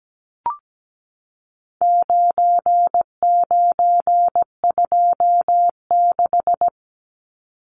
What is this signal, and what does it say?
Morse code, heard as a keyed tone, about 17 wpm, 700 Hz. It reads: 9926
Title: Numbers - 4-Digit 17wpm